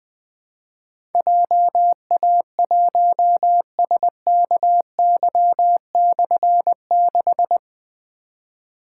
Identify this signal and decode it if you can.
Morse code, heard as a keyed tone, about 20 wpm, 700 Hz. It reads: JA1SKY/6
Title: Difficult Call Signs 20wpm